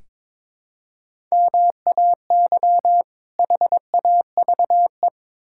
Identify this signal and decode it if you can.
Morse code, heard as a keyed tone, about 22 wpm, 700 Hz. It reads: MAY HAVE
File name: Build A Phrase - 2-Word Phrases - 22wpm